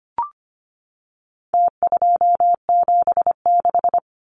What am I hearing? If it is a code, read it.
Morse code, heard as a keyed tone, about 25 wpm, 700 Hz. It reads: T276
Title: Numbers - Contest Style - 4-Digit 25wpm